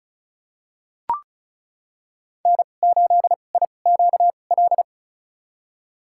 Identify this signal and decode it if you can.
Morse code, heard as a keyed tone, about 36 wpm, spaced as for 25 wpm, 700 Hz. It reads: N8IQL
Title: USA Call Signs - 5 Characters - 2x Character Spacing - 35wpm